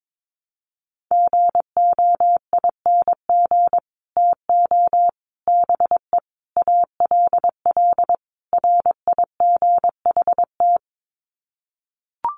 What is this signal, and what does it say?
Morse code, heard as a keyed tone, about 22 wpm, 700 Hz. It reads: GOING TO BE ALL RIGHT
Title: Sets of 5 Words Encore2 22wpm